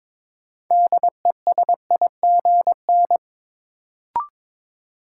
Morse code, heard as a keyed tone, about 22 wpm, 700 Hz. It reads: DESIGN
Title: Sentences from Top 800 Words - New Word Drill - 22wpm